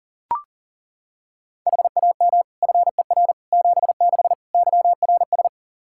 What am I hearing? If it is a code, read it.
Morse code, heard as a keyed tone, about 37 wpm, 690 Hz. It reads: HAM FER 76 YRS
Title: Practice QSO - Exchange 3 - More Info - 40wpm